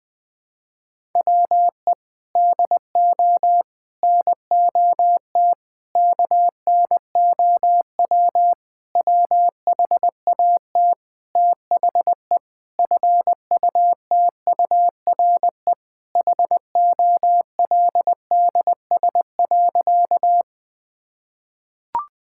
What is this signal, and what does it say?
Morse code, heard as a keyed tone, about 20 wpm, 700 Hz. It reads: WE DO NOT KNOW WHAT THE FUTURE HOLDS.
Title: Sentences with 5-Word Phrases - 20wpm